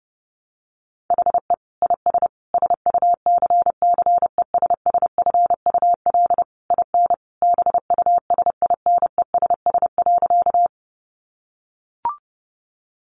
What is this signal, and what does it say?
Morse code, heard as a keyed tone, about 30 wpm, 700 Hz. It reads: HE IS SUCCESSFUL IN BUSINESS.
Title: Sentences from Top 900 Words 30wpm